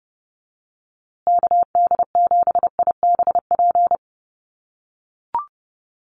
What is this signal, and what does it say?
Morse code, heard as a keyed tone, about 28 wpm, 700 Hz. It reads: KD7IBP
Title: USA Call Signs Repeated 3x 30wpm